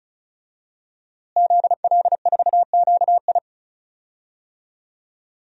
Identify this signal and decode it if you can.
Morse code, heard as a keyed tone, about 34 wpm, 700 Hz. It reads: ZL4QI
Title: DXCC-100 Call Signs - 5 Characters - Repeated 3x - 35wpm